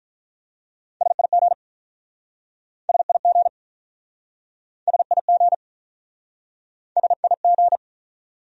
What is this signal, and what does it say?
Morse code, heard as a keyed tone, about 43 wpm, 700 Hz. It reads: SIG SIG SIG SIG
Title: Speed-Racing Abbreviations - 35wpm